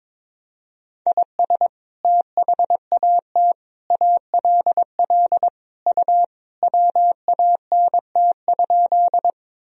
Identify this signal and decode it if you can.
Morse code, heard as a keyed tone, about 22 wpm, 700 Hz. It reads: IS THAT ALL U WANT?
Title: Sentences from Top 200 Words and Abbreviations 22wpm